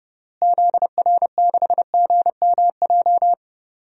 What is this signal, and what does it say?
Morse code, heard as a keyed tone, about 30 wpm, 700 Hz. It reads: ZR6GMJ
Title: DXCC-100 Call Signs - 6 Characters - Repeated 3x - 30wpm